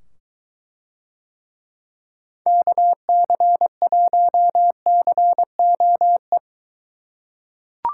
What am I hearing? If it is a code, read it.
Morse code, heard as a keyed tone, about 23 wpm, 700 Hz. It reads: KC1COE